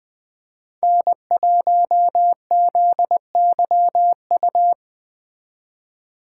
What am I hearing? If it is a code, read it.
Morse code, heard as a keyed tone, about 20 wpm, 700 Hz. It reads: N1ZYU
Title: Speed Racing - USA 1x3 Call Signs 17wpm